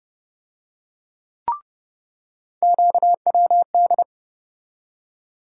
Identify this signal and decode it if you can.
Morse code, heard as a keyed tone, about 30 wpm, 700 Hz. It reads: QWD
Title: USA Call Sign Postfixes - 3 Letters 30wpm